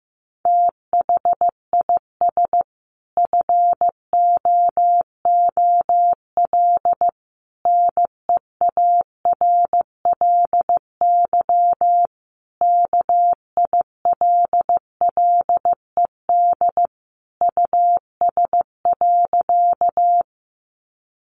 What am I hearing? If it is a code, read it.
Morse code, heard as a keyed tone, about 14 wpm, 700 Hz. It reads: THIS FOOL NEARLY KILLED US.
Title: A Sound Of Thunder 15wpm